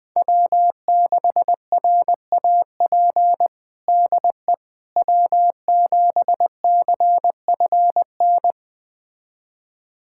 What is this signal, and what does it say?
Morse code, heard as a keyed tone, about 20 wpm, 700 Hz. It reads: W6RAP DE W7CFN
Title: USA Callsign DE Callsign - 5 Characters - 20wpm